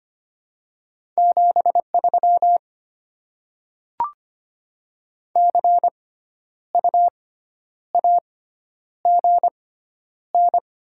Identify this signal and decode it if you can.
Morse code, heard as a keyed tone, about 25 wpm, spaced as for 10 wpm, 700 Hz. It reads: 73 CU AGN
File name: Top 20 QSO Elements - 6x Character Spacing - 25wpm